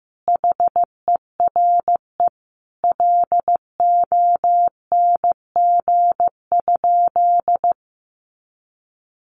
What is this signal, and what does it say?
Morse code, heard as a keyed tone, about 15 wpm, 700 Hz. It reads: HERE LONG?
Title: Sentences from Top 200 Words 15wpm